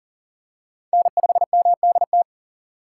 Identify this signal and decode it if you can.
Morse code, heard as a keyed tone, about 40 wpm, 700 Hz. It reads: N5MDT
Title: USA Call Signs - 5 Characters - 6x Character Spacing - 40wpm